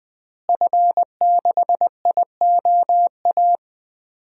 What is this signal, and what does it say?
Morse code, heard as a keyed tone, about 20 wpm, 700 Hz. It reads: F6IOA
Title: DXCC-100 Call Signs - 5 Characters - Repeated 3x - 20wpm